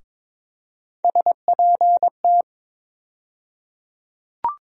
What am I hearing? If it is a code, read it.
Morse code, heard as a keyed tone, about 22 wpm, 700 Hz. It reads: SPT
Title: USA Call Sign Postfixes - 3 Letters 22wpm